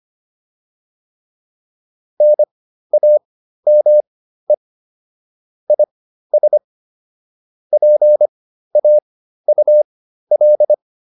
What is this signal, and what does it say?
Morse code, heard as a keyed tone, about 25 wpm, spaced as for 13 wpm, 600 Hz. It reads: NAME IS PAUL
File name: QSO 407 13